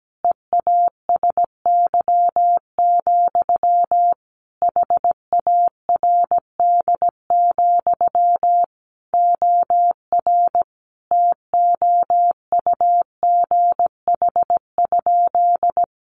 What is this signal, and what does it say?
Morse code, heard as a keyed tone, about 17 wpm, 700 Hz. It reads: EASY, HARD, OR TOUGH?
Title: Sentences from Top 1200 Words 17wpm